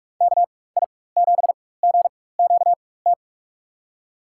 Morse code, heard as a keyed tone, about 43 wpm, spaced as for 21 wpm, 700 Hz. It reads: KI7GQT
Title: USA Call Signs - 6 Characters - 4x Character Spacing - 45wpm